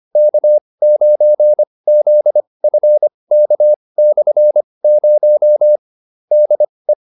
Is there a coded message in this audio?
Yes: Morse code from 0.2 to 6.9 s, heard as a keyed tone, about 25 wpm, spaced as for 20 wpm, 600 Hz. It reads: K9ZFK/0 DE